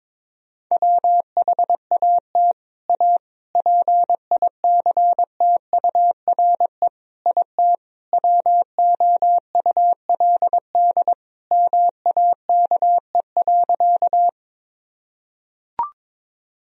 Morse code, heard as a keyed tone, about 21 wpm, 700 Hz. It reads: WHAT A PICTURE IT WOULD MAKE.
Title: Sentences from Top 1000 Words - Review Part 1 - 22wpm